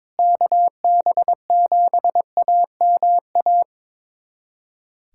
Morse code, heard as a keyed tone, about 22 wpm, 700 Hz. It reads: KB7AMA
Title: USA Callsign DE Callsign - 6 Characters - 22wpm